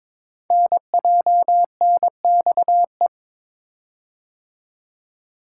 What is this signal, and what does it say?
Morse code, heard as a keyed tone, about 22 wpm, 700 Hz. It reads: NJNXE
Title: Code Groups - 5 Letters - 22wpm